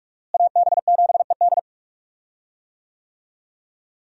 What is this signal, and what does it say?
Morse code, heard as a keyed tone, about 42 wpm, 700 Hz. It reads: AB7ED
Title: USA Call Signs - 5 Characters - Repeated 3x - 45wpm